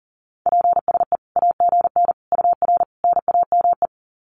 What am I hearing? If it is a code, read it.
Morse code, heard as a keyed tone, about 40 wpm, 700 Hz. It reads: PSE AGN UR NAME
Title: Common QSO Phrases 40wpm